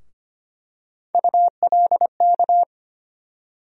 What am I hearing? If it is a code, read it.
Morse code, heard as a keyed tone, about 25 wpm, 700 Hz. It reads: ULK
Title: USA Call Sign Postfixes - 3 Letters 25wpm